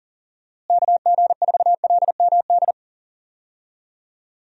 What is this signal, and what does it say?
Morse code, heard as a keyed tone, about 40 wpm, 700 Hz. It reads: KG4LMD
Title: USA Call Signs - 6 Characters - Repeated 2x - 40wpm